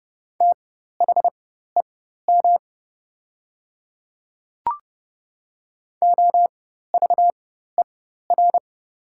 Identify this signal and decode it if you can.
Morse code, heard as a keyed tone, about 30 wpm, spaced as for 14 wpm, 700 Hz. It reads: THEM OVER
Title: 4x Character Spacing - Top 100 Words - 30wpm